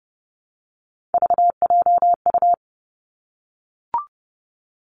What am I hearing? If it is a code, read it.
Morse code, heard as a keyed tone, about 30 wpm, 700 Hz. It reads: VJU
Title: Three Challenging Letter-Number Combinations 30wpm